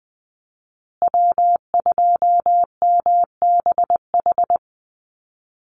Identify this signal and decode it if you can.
Morse code, heard as a keyed tone, about 20 wpm, 700 Hz. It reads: W2MBH
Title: USA Call Signs Repeated 2x 20wpm